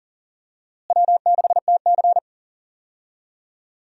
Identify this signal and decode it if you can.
Morse code, heard as a keyed tone, about 40 wpm, 700 Hz. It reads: W6TC
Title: USA Call Signs - 4 Characters - Repeated 2x - 40wpm